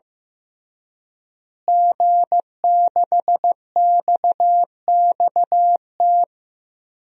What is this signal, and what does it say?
Morse code, heard as a keyed tone, about 16 wpm, 700 Hz. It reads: G6XXT
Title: DXCC-100 Call Signs - 5 Characters - Repeated 2x - 15wpm